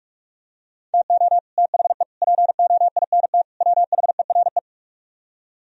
Morse code, heard as a keyed tone, about 45 wpm, 700 Hz. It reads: TO THE POINT WHERE